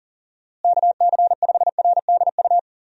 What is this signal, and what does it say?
Morse code, heard as a keyed tone, about 40 wpm, 700 Hz. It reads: KC5RDU
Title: USA Call Signs - 6 Characters - Repeated 3x - 40wpm